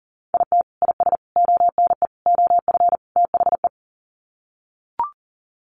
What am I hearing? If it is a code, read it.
Morse code, heard as a keyed tone, about 37 wpm, 700 Hz. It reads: IT IS ONE OF THE
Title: Sets of 5 Words 40wpm